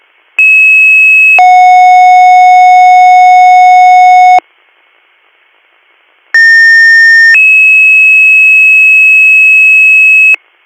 Two-tone page, heard as a one-second tone, then a three-second tone, about 2600 then 730 Hz, about 1800 then 2600 Hz. It is heard from 0.4 to 10.3 s.